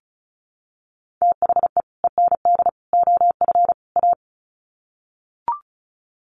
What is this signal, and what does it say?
Morse code, heard as a keyed tone, about 34 wpm, 700 Hz. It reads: THE END OF A
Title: Sets of 4 Words 35wpm